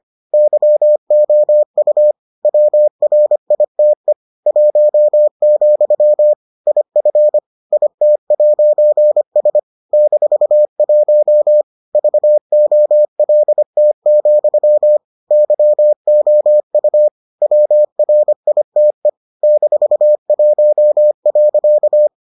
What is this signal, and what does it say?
Morse code, heard as a keyed tone, about 25 wpm, 600 Hz. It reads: YOU WRITE 1, IF IT'S -1 VOLT, YOU WRITE -1.